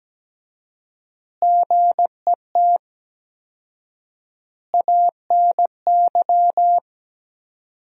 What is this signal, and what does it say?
Morse code, heard as a keyed tone, about 17 wpm, 700 Hz. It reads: GET ANY
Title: 4x Word-Spacing - Sentences from Top 400 Words - 17wpm